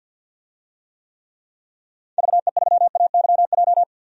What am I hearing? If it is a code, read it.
Morse code, heard as a keyed tone, about 47 wpm, 700 Hz. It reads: VE3AYJ
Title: DXCC-100 Call Signs - 6 Characters - 50wpm